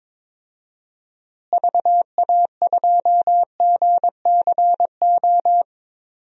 Morse code, heard as a keyed tone, about 22 wpm, 700 Hz. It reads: VA2GCO
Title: DXCC-100 Call Signs - 6 Characters - Repeated 2x - 22wpm